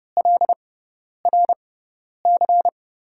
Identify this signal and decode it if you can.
Morse code, heard as a keyed tone, about 30 wpm, spaced as for 10 wpm, 700 Hz. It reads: LRC
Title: USA Call Signs - 6 Characters - 6x Character Spacing - 30wpm